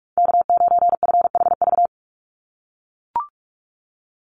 Morse code, heard as a keyed tone, about 44 wpm, 700 Hz. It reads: K9FHV
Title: USA Call Signs 45wpm